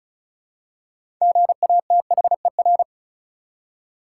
Morse code, heard as a keyed tone, about 36 wpm, 700 Hz. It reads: GATHER
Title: Six Letter Words - 35wpm